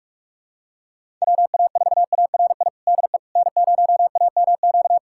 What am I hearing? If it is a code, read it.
Morse code, heard as a keyed tone, about 46 wpm, 690 Hz. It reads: WA4ARI DE N0AMQ
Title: USA Callsign DE Callsign 45wpm